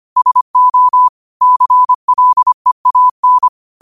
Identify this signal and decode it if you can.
Morse code, heard as a keyed tone, about 25 wpm, 1000 Hz. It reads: SO CLEAN